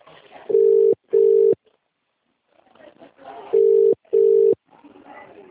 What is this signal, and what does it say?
Ringback tone (UK), 400 + 450 Hz.